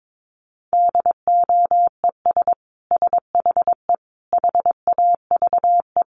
Morse code, heard as a keyed tone, about 22 wpm, 700 Hz. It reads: DOES SHE HAVE
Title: Sentences from Top 800 Words 22wpm